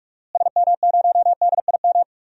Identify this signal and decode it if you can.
Morse code, heard as a keyed tone, about 46 wpm, 700 Hz. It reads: SM0DIM
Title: DXCC-100 Call Signs - 6 Characters - Repeated 2x - 45wpm